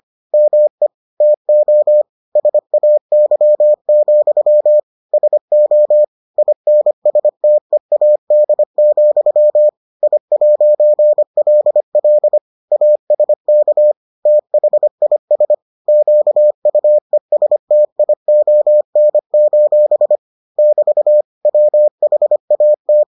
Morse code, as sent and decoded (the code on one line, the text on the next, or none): -- . / - --- / ... .- -.-- --..-- / ... --- / .. -. ... - . .- -.. --..-- / .. .----. .-.. .-.. / .- ... -.- / - .... .. ... / --.- ..- . ... - .. --- -. ---... / -...- / .-- .... .- -
ME TO SAY, SO INSTEAD, I'LL ASK THIS QUESTION: <BT> WHAT